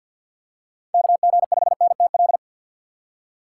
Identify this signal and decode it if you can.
Morse code, heard as a keyed tone, about 52 wpm, 700 Hz. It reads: KG5NTL
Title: USA Call Signs - 6 Characters - 4x Character Spacing - 50wpm